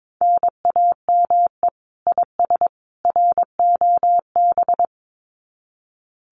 Morse code, heard as a keyed tone, about 22 wpm, 700 Hz. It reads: NAME IS ROB
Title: Common QSO Phrases 22wpm